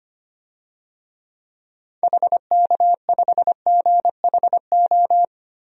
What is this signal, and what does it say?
Morse code, heard as a keyed tone, about 25 wpm, 700 Hz. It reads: HK5GHO
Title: DXCC-100 Call Signs - 6 Characters - 25wpm